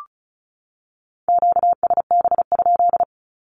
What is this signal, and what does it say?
Morse code, heard as a keyed tone, about 34 wpm, 700 Hz. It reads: QSB?
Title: Q-Codes 35wpm